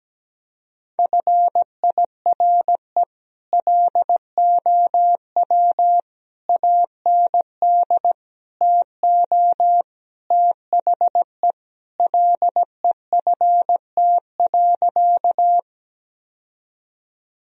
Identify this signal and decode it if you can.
Morse code, heard as a keyed tone, about 17 wpm, 700 Hz. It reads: FIRE LOW AND TO THE LEFT.
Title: Sentences from Top 1000 Words - Review Part 1 - 17wpm